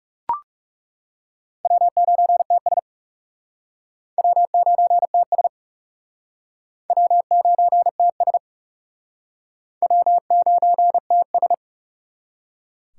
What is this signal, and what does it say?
Morse code, heard as a keyed tone, about 37 wpm, 700 Hz. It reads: W9TS W9TS W9TS W9TS
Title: Speed Racing - USA Call Signs 30wpm